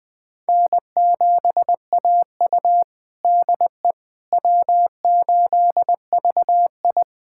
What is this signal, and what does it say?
Morse code, heard as a keyed tone, about 21 wpm, 700 Hz. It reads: N7AU DE W8VI
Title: USA Callsign DE Callsign - 4 Characters - 20wpm